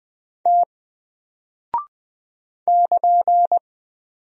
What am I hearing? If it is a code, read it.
Morse code, heard as a keyed tone, about 20 wpm, 700 Hz. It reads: T <KN>
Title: Single Character (Letter:Number:Punctuation:Prosign) Alt 20wpm